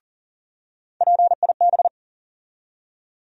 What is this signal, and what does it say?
Morse code, heard as a keyed tone, about 40 wpm, 700 Hz. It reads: PIB
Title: USA Call Sign Postfixes - 3 Letters 40wpm